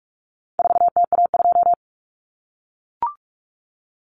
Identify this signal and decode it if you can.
Morse code, heard as a keyed tone, about 44 wpm, 700 Hz. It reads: 4TA2
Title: Common Cut Numbers - 4-Digit 45wpm